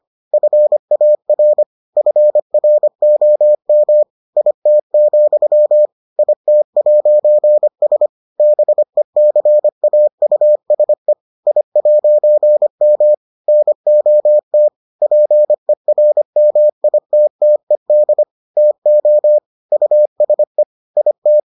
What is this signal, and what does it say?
Morse code, heard as a keyed tone, about 25 wpm, 600 Hz. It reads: FAR FROM IT, IT'S BECAUSE I'M NOT PERMITTED TO USE IT